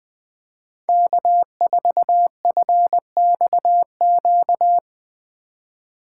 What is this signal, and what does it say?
Morse code, heard as a keyed tone, about 23 wpm, 700 Hz. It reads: K4FXQ
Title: Speed Racing - USA 1x3 Call Signs 20wpm